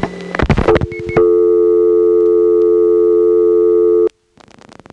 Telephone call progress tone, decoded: dial tone